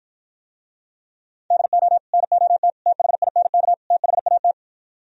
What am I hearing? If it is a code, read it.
Morse code, heard as a keyed tone, about 50 wpm, 700 Hz. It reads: DO NOT THINK THAT